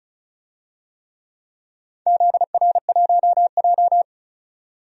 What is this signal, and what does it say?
Morse code, heard as a keyed tone, about 34 wpm, 700 Hz. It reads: ZR1J
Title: DXCC-100 Call Signs - 4 Characters - 35wpm